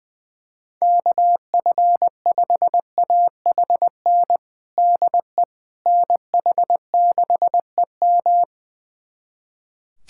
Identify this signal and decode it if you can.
Morse code, heard as a keyed tone, about 20 wpm, 700 Hz. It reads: KF5AHN DE NH6EM
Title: USA Callsign DE Callsign 20wpm